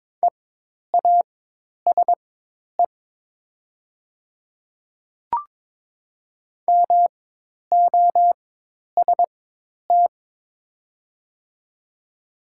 Morse code, heard as a keyed tone, about 22 wpm, spaced as for 10 wpm, 700 Hz. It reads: EASE MOST
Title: 4x Character Spacing - Top 300 Words - 22wpm